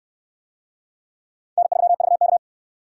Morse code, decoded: N3VK

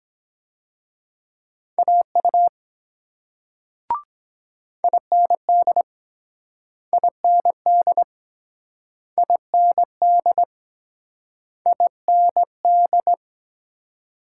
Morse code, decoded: AU IND IND IND IND

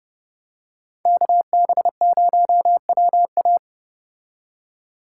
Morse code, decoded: KB0WA

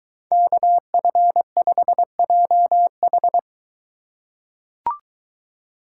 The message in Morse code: KF5JH

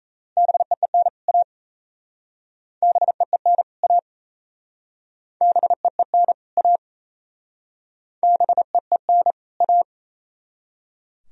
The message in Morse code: BEEN A BEEN A BEEN A BEEN A

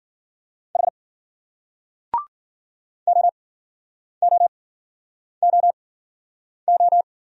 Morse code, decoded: H OOOO

